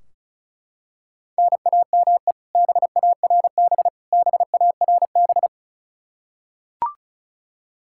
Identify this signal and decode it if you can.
Morse code, heard as a keyed tone, about 36 wpm, 700 Hz. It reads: NAME BARB BARB
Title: Practice QSO - Exchange 1 - Beginning - 35wpm